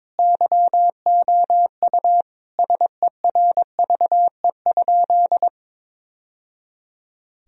Morse code, heard as a keyed tone, about 21 wpm, 700 Hz. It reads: YOU SERVE?
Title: Sentences from Top 1000 Words - Review Part 2 - 22wpm